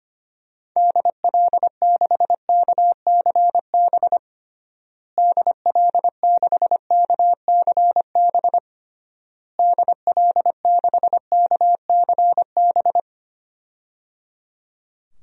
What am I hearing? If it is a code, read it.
Morse code, heard as a keyed tone, about 25 wpm, 700 Hz. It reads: DL6KCB DL6KCB DL6KCB
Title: DXCC-100 Call Signs - 6 Characters - Repeated 3x - 25wpm